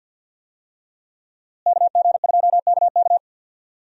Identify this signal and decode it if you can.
Morse code, heard as a keyed tone, about 47 wpm, 700 Hz. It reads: KG2KK
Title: USA Call Signs - 5 Characters - Repeated 3x - 50wpm